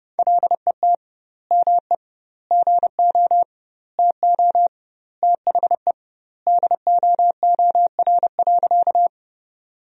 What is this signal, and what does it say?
Morse code, heard as a keyed tone, about 31 wpm, 700 Hz. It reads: LET ME GO TO THE DOOR.
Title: 2x Word-Spacing - Sentences from Top 300 Words - 30wpm